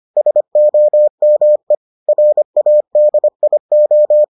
Morse code, heard as a keyed tone, about 25 wpm, 600 Hz. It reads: SOME RADIO